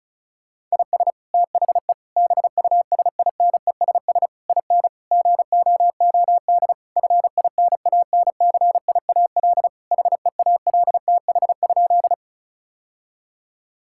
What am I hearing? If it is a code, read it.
Morse code, heard as a keyed tone, about 36 wpm, 700 Hz. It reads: IS THE BUSINESS IN GOOD FINANCIAL HEALTH?
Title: Sentences from Top 1000 Words - Review Part 5 - 35wpm